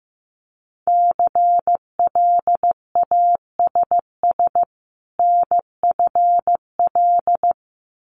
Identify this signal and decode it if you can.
Morse code, heard as a keyed tone, about 15 wpm, 700 Hz. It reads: CLASS NFL
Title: Winter Field Day Exchanges 15wpm